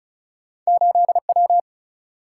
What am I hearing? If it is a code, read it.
Morse code, heard as a keyed tone, about 36 wpm, 700 Hz. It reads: 8W
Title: Single-Letter Single-Number 35wpm